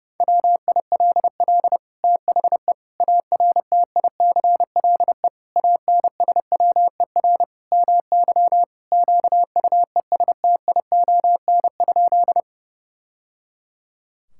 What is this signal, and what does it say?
Morse code, heard as a keyed tone, about 32 wpm, 700 Hz. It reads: WILL THE ARTICLE ANSWER MY QUESTION?
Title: Sentences from Top 1000 Words - Review Part 5 - 30wpm